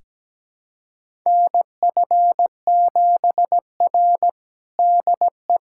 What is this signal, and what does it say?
Morse code, heard as a keyed tone, about 17 wpm, 700 Hz. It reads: NF7R DE
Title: USA Callsign DE Callsign - 4 Characters - 17wpm